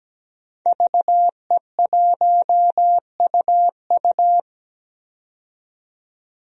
Morse code, heard as a keyed tone, about 17 wpm, 700 Hz. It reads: VE1UU
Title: DXCC-100 Call Signs - 5 Characters - 17wpm